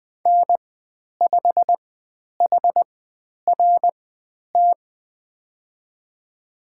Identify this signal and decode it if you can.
Morse code, heard as a keyed tone, about 20 wpm, spaced as for 10 wpm, 700 Hz. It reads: N5HRT
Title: 095 Lesson 16g - Letter c - Call Signs 20wpm